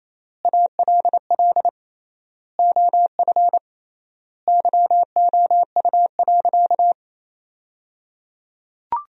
Morse code, heard as a keyed tone, about 28 wpm, 700 Hz. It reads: ALL OF YOU.
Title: 3x Word-Spacing - Sentences from Top 500 Words - 28wpm